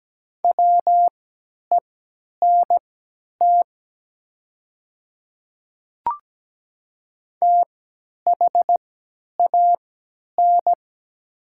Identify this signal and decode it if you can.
Morse code, heard as a keyed tone, about 17 wpm, spaced as for 10 wpm, 700 Hz. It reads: WENT THAN
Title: 3x Character Spacing - Top 200 Words - 17wpm